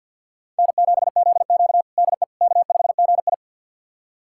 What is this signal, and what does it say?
Morse code, heard as a keyed tone, about 47 wpm, 700 Hz. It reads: N7ZQ DE K5GI
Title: USA Callsign DE Callsign - 4 Characters - 50wpm